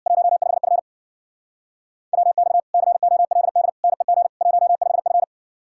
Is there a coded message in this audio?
Yes: Morse code from 0.1 to 5.2 s, heard as a keyed tone, about 63 wpm, 700 Hz. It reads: 15F WX COLD NEG 15F